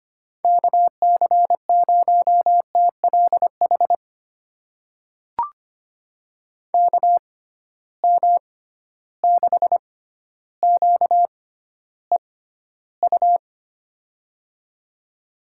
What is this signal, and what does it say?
Morse code, heard as a keyed tone, about 25 wpm, spaced as for 13 wpm, 700 Hz. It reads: KC0TLH KM6QEU
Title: USA Call Signs - 6 Characters - 6x Character Spacing - 25wpm